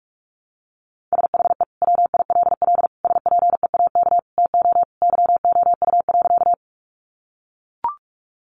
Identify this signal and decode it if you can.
Morse code, heard as a keyed tone, about 44 wpm, 700 Hz. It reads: SHE WILL SPEAK TO YOU.
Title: Sentences from Top 1100 Words 45wpm